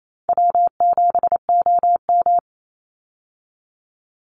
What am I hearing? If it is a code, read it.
Morse code, heard as a keyed tone, about 28 wpm, 700 Hz. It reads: W7OM
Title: Easy Call Signs Repeated 2x - 28wpm